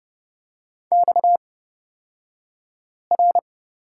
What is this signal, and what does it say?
Morse code, heard as a keyed tone, about 30 wpm, spaced as for 5 wpm, 700 Hz. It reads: XR
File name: Lesson 37d - Letter x - Two Characters 30wpm